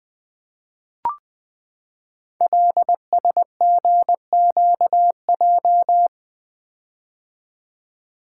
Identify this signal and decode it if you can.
Morse code, heard as a keyed tone, about 20 wpm, 700 Hz. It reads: LSGQJ
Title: Code Groups - 5 Letter:Numbers - 20wpm